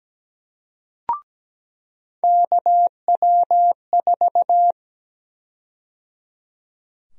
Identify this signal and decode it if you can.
Morse code, heard as a keyed tone, about 17 wpm, 700 Hz. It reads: KW4
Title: USA Call Sign Prefixes - 17wpm